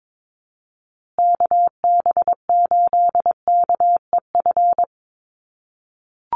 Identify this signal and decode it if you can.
Morse code, heard as a keyed tone, about 22 wpm, 700 Hz. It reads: KB8KEF
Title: USA Call Signs Repeated 3x 22wpm